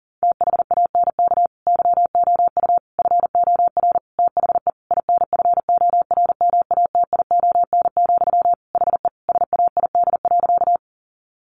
Morse code, heard as a keyed tone, about 37 wpm, 700 Hz. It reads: THANK YOU FOR THE INFORMATION, HE SAID.
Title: Sentences from Top 600 Words 40wpm